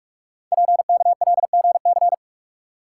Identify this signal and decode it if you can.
Morse code, heard as a keyed tone, about 45 wpm, 690 Hz. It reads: PKLGC
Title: Code Groups - 5 Letter:Numbers - 45wpm